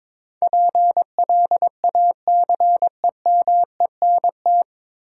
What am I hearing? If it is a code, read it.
Morse code, heard as a keyed tone, about 22 wpm, 700 Hz. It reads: PLACEMENT
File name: 9 Letter Words - First 3 Letters Given - 22wpm